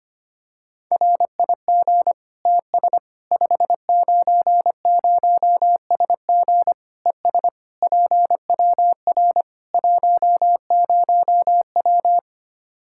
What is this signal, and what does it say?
Morse code, heard as a keyed tone, about 25 wpm, 700 Hz. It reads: RIG TS 590SG ES PWR 10W
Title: Practice QSO - Exchange 2 - Info - 25wpm